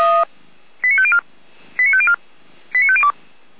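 Five-tone selective call (ZVEI), heard as quick runs of five tones, tones of about 70 ms. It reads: 89504 89504 89502
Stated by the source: ZVEI five-tone call 89108 received by SDR, recorded by feuerwehr-orlamuende.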